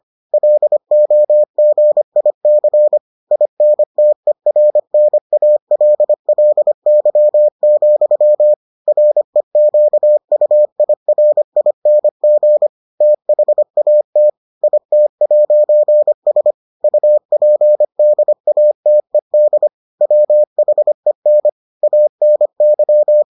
Morse code, heard as a keyed tone, about 25 wpm, 600 Hz. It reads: LOGIC INTERNALLY, REQUIRING THAT IT'S UPDATED WHEN ANY